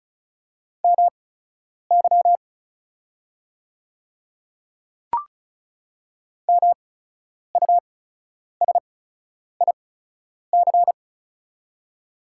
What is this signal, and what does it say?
Morse code, heard as a keyed tone, about 34 wpm, spaced as for 10 wpm, 700 Hz. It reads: MY MUSIC
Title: 8x Character Spacing - Top 300 Words - 35wpm